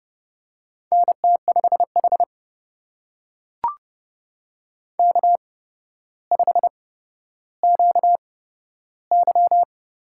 Morse code, heard as a keyed tone, about 30 wpm, spaced as for 13 wpm, 700 Hz. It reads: NT5H K5QY
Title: USA Call Signs - 8x Character Spacing - 30wpm